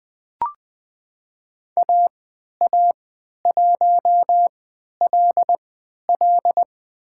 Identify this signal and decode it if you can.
Morse code, heard as a keyed tone, about 20 wpm, spaced as for 11 wpm, 700 Hz. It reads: AA1LL
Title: USA Call Signs - 5 Characters - 3x Character Spacing - 20wpm